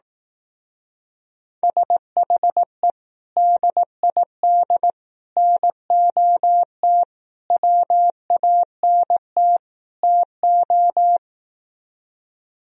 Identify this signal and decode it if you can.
Morse code, heard as a keyed tone, about 18 wpm, 700 Hz. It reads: SHE DID NOT WANT TO
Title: Speed-Racing - Sets of 5 Words 15wpm